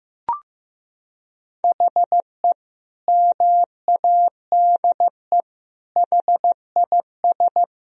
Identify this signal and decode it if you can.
Morse code, heard as a keyed tone, about 15 wpm, 700 Hz. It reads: HE MADE HIS
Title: Sentences from Top 1000 Words - Review Part 4 - 15wpm